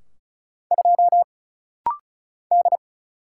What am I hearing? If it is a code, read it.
Morse code, heard as a keyed tone, about 36 wpm, 700 Hz. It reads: 2 D
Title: Single Letter-Number - Rapid-Fire 35wpm